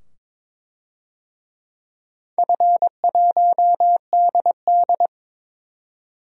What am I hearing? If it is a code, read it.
Morse code, heard as a keyed tone, about 22 wpm, 700 Hz. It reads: F1DD
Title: DXCC-100 Call Signs - 4 Characters - 22wpm